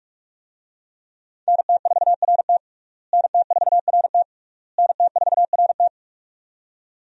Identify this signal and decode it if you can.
Morse code, heard as a keyed tone, about 46 wpm, 700 Hz. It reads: NT4RT NT4RT NT4RT
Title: USA Call Signs - 5 Characters - Repeated 3x - 45wpm